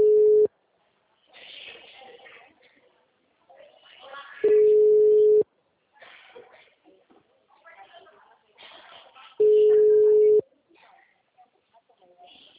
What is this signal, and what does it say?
Ringback tone (Europe, 425 Hz), with speech.